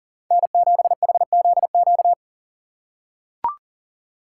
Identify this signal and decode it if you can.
Morse code, heard as a keyed tone, about 41 wpm, 690 Hz. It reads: N7HZQ
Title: USA Call Signs - 5 Characters - 6x Character Spacing - 40wpm